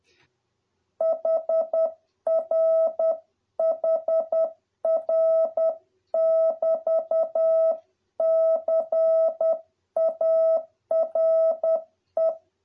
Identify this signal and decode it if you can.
Morse code, heard as a keyed tone, about 10 wpm, 650 Hz. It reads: HR HR <BT> CARE